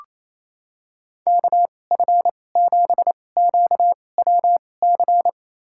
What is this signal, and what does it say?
Morse code, heard as a keyed tone, about 28 wpm, spaced as for 20 wpm, 700 Hz. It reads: KF7QWC
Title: USA Call Signs - 6 Characters - 2x Character Spacing - 28wpm